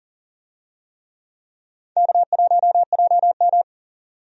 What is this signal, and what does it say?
Morse code, heard as a keyed tone, about 40 wpm, 700 Hz. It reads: K1JM
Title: USA Call Signs - 4 Characters - 40wpm